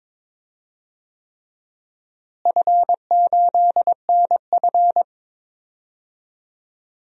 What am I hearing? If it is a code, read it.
Morse code, heard as a keyed tone, about 22 wpm, 700 Hz. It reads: F8NF
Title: DXCC-100 Call Signs - 4 Characters - Repeated 3x - 22wpm